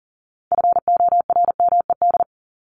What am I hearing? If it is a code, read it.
Morse code, heard as a keyed tone, about 40 wpm, 700 Hz. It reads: FORMED
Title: ED Suffix - Top 100 Words with Base Words 40wpm